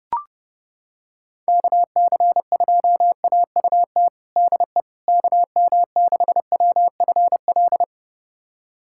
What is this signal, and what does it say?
Morse code, heard as a keyed tone, about 30 wpm, 700 Hz. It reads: KC2AUT DE KM6WFL
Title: USA Callsign DE Callsign - 6 Characters - 30wpm